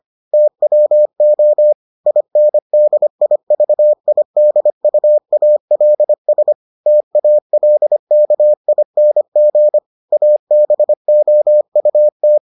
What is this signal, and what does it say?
Morse code, heard as a keyed tone, about 25 wpm, 600 Hz. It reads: TWO INDIVIDUALS TALKING ABOUT